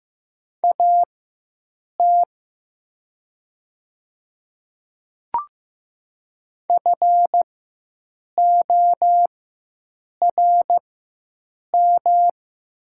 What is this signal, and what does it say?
Morse code, heard as a keyed tone, about 15 wpm, spaced as for 7 wpm, 700 Hz. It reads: AT FORM
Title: 4x Character Spacing - Top 200 Words - 15wpm